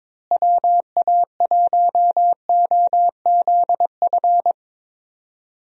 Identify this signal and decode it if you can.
Morse code, heard as a keyed tone, about 21 wpm, 700 Hz. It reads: WA1OZF
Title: USA Call Signs - 6 Characters - Repeated 3x - 22wpm